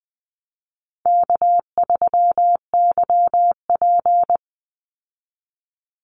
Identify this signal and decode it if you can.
Morse code, heard as a keyed tone, about 20 wpm, 700 Hz. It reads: K3YP
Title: Call Signs Repeated 2x - 20wpm